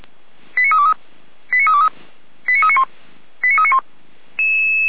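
Five-tone selective call (ZVEI), heard as quick runs of five tones, tones of about 70 ms. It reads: 89323 89323 89391 89391
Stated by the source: ZVEI five-tone call 89501 received on SDR.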